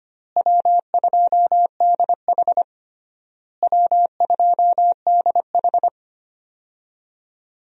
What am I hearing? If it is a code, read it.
Morse code, heard as a keyed tone, about 25 wpm, 700 Hz. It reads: W2DH W2DH
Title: USA Call Signs - 4 Characters - Repeated 3x - 25wpm